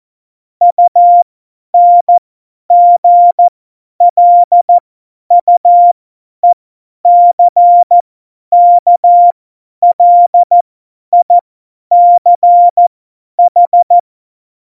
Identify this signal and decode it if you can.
Morse code, heard as a keyed tone, about 14 wpm, spaced as for 10 wpm, 700 Hz. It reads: UNGLUECKLICH
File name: Morsekurs / Witze / Witz 27 March 2018, 50 F70